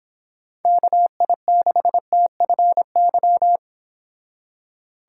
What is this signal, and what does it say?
Morse code, heard as a keyed tone, about 26 wpm, 700 Hz. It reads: KI6TFY
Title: Speed Racing - USA 2x3 Call Signs 22wpm